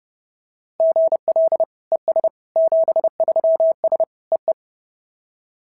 Morse code, heard as a keyed tone, about 28 wpm, 660 Hz. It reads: GL ES 73S EE
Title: SOTA QSO Practice 30wpm